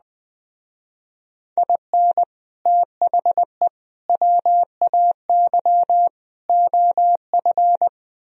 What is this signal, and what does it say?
Morse code, heard as a keyed tone, about 20 wpm, 700 Hz. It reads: IN THE WAY OF
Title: Speed-Racing - Sets of 4 Words 17wpm